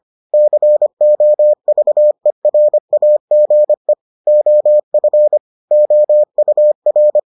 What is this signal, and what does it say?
Morse code, heard as a keyed tone, about 25 wpm, 600 Hz. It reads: COVERAGE OF OUR